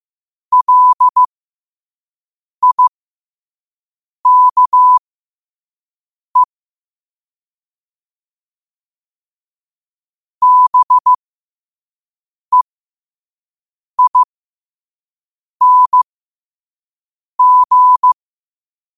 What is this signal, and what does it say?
Morse code, heard as a keyed tone, about 15 wpm, spaced as for 5 wpm, 1000 Hz. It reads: LIKE BEING
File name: qotdfiles / 5wpm